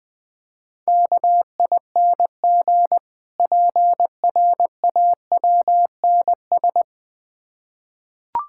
Morse code, heard as a keyed tone, about 20 wpm, 690 Hz. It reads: KING PRAWNS
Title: Difficult Words with Hints - 20wpm